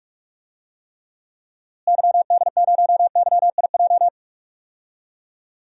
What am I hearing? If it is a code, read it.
Morse code, heard as a keyed tone, about 46 wpm, 700 Hz. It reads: YD0YIJ
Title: DXCC-100 Call Signs - 6 Characters - Repeated 3x - 45wpm